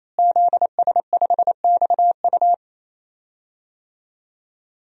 Morse code, heard as a keyed tone, about 27 wpm, 700 Hz. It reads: ZS5XU
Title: DXCC-100 Call Signs - 5 Characters - Repeated 3x - 28wpm